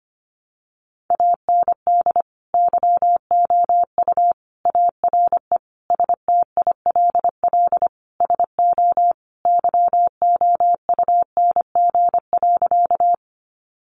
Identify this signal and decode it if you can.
Morse code, heard as a keyed tone, about 25 wpm, 700 Hz. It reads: AND YOU ARE STILL SO YOUNG.
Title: Sentences from Top 400 Words 25wpm